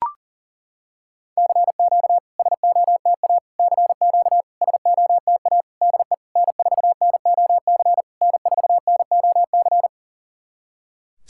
Morse code, heard as a keyed tone, about 40 wpm, 700 Hz. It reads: CQ SOTA CQ SOTA DE N4NOC N4NOC
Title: SOTA QSO Practice 40wpm